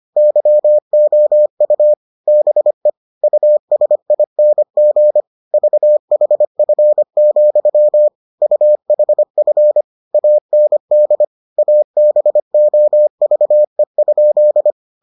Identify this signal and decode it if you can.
Morse code, heard as a keyed tone, about 25 wpm, 600 Hz. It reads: YOU BE USING VHF, UHF AND ABOVE?